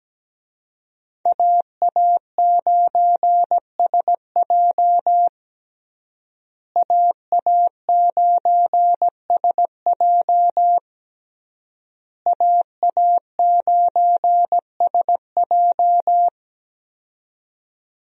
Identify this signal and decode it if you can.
Morse code, heard as a keyed tone, about 17 wpm, 700 Hz. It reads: AA9SJ AA9SJ AA9SJ
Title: USA Call Signs - 5 Characters - Repeated 3x - 17wpm